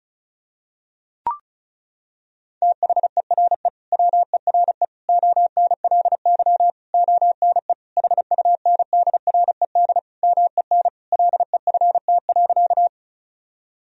Morse code, heard as a keyed tone, about 36 wpm, 700 Hz. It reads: THERE WERE ONLY ONE HUNDRED MEN LEFT.
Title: Sentences from Top 1000 Words - Review Part 2 - 35wpm